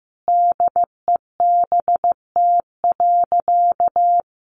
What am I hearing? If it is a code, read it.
Morse code, heard as a keyed tone, about 15 wpm, 700 Hz. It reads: DEBT.
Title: Sentences from Top 1300 Words 15wpm